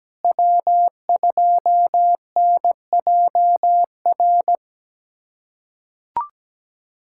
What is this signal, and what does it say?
Morse code, heard as a keyed tone, about 17 wpm, 700 Hz. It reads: W2NJR
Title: USA Callsign DE Callsign - 5 Characters - 17wpm